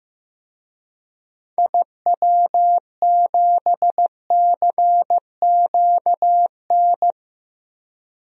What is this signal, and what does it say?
Morse code, heard as a keyed tone, about 15 wpm, 700 Hz. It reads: IW7CQN
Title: DXCC-100 Call Signs - 6 Characters - 15wpm